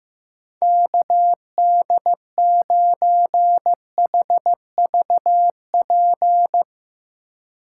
Morse code, heard as a keyed tone, about 16 wpm, 700 Hz. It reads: KD9HVP
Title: Speed Racing - USA 2x3 Call Signs 15wpm